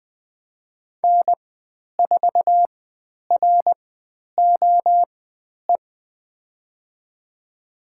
Morse code, heard as a keyed tone, about 20 wpm, spaced as for 10 wpm, 700 Hz. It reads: N4ROE
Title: 095 Lesson 16g - Letter c - Call Signs 20wpm